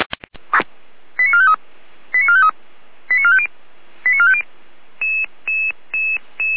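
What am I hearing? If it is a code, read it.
Five-tone selective call (ZVEI), heard as quick runs of five tones, tones of about 70 ms. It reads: 89453 89453 89450 89450
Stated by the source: ZVEI five-tone selective call received on SDR.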